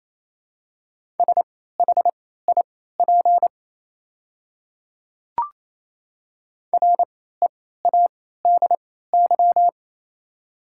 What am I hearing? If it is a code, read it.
Morse code, heard as a keyed tone, about 27 wpm, spaced as for 16 wpm, 700 Hz. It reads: SHIP READY
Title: 3x Character Spacing - Top 500 Words - 28wpm